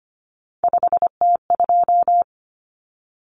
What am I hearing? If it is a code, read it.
Morse code, heard as a keyed tone, about 25 wpm, 700 Hz. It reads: 5T2
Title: Common Cut Numbers - 3-Digit 25wpm